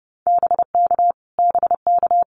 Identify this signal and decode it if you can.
Morse code, heard as a keyed tone, about 30 wpm, 700 Hz. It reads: BK BK
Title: Abbreviations Repeated 3x - 30wpm